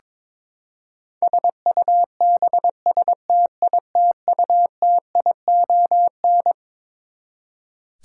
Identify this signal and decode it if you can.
Morse code, heard as a keyed tone, about 22 wpm, 700 Hz. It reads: SUBSTITUTION
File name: Twelve Letter Words - 22wpm